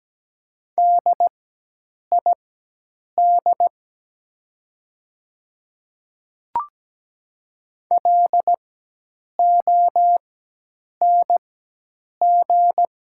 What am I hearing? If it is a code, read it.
Morse code, heard as a keyed tone, about 17 wpm, spaced as for 8 wpm, 700 Hz. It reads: DID LONG
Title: 4x Character Spacing - Top 100 Words - 17wpm